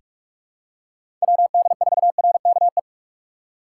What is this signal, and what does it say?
Morse code, heard as a keyed tone, about 46 wpm, 700 Hz. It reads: WD4RKE
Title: USA Call Signs - 6 Characters - 4x Character Spacing - 45wpm